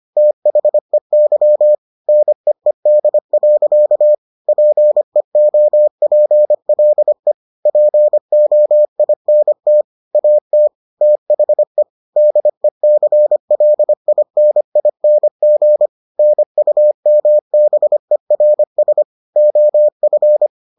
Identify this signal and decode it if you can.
Morse code, heard as a keyed tone, about 25 wpm, 600 Hz. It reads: THEY NEED. PEOPLE POINT AT THE DECLINING NUMBERS OF